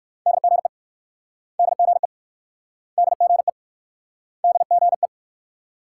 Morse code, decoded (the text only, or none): DGE DGE DGE DGE